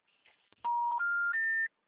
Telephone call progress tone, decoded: special information tone

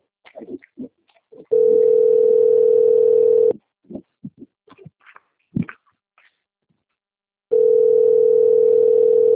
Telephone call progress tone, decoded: ringback tone

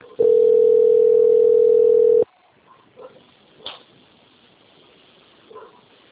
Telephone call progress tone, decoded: ringback tone

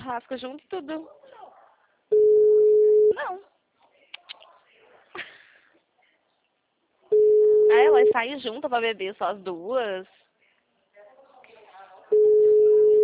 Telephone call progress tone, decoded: ringback tone

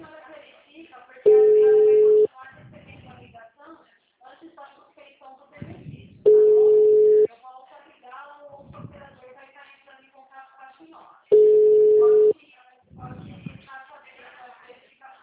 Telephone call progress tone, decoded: ringback tone

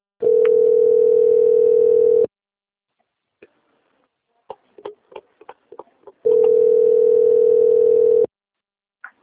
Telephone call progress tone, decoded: ringback tone

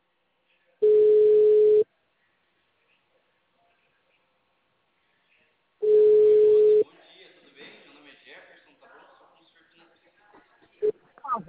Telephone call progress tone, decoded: ringback tone

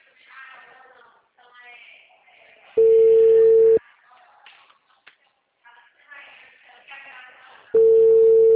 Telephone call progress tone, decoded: ringback tone

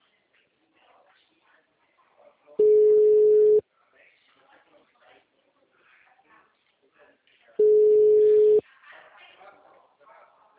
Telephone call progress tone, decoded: ringback tone

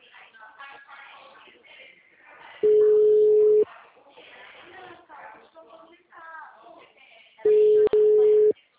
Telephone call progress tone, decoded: ringback tone

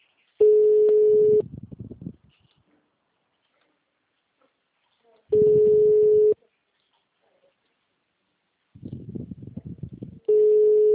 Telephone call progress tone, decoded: ringback tone

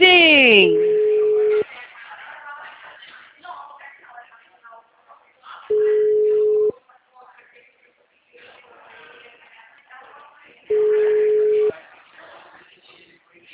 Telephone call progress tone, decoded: ringback tone

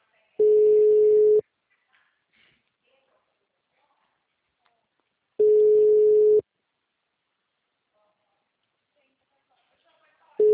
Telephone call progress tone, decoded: ringback tone

